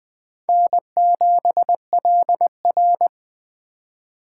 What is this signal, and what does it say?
Morse code, heard as a keyed tone, about 20 wpm, 700 Hz. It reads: N7LR
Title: K1USN SST QSO Practice - 20wpm